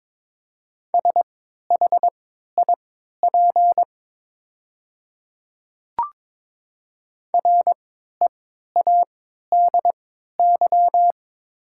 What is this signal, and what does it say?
Morse code, heard as a keyed tone, about 22 wpm, spaced as for 13 wpm, 690 Hz. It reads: SHIP READY
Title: 3x Character Spacing - Top 500 Words - 22wpm